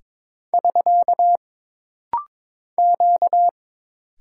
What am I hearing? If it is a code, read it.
Morse code, heard as a keyed tone, about 22 wpm, 700 Hz. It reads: <SK> Q
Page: This rapid-fire timing is geared to push your ICR (Instant Character Recognition) skills.